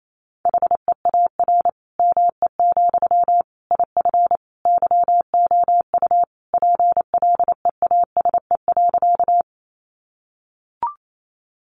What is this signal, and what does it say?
Morse code, heard as a keyed tone, about 28 wpm, 700 Hz. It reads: HEAR ME, IF YOU PLEASE.
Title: Sentences from Top 1500 Words 28wpm